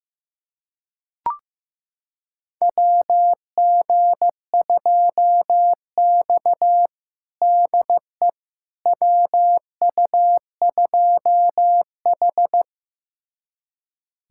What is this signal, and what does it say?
Morse code, heard as a keyed tone, about 16 wpm, 700 Hz. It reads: WG2X DE WU2H
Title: USA Callsign DE Callsign - 4 Characters - 15wpm